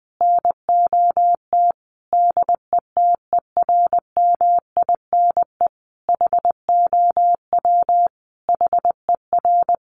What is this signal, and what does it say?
Morse code, heard as a keyed tone, about 19 wpm, 700 Hz. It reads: NOT DETERMINE HOW HER
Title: Sentences from Top 700 Words 20wpm